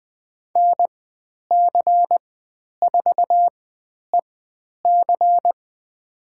Morse code, heard as a keyed tone, about 20 wpm, spaced as for 10 wpm, 700 Hz. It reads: NC4EC
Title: 102 Lesson 17g - Letter u - Call Signs 20wpm